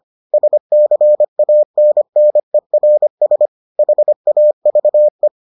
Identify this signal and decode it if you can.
Morse code, heard as a keyed tone, about 25 wpm, 600 Hz. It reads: SCANNERS HAVE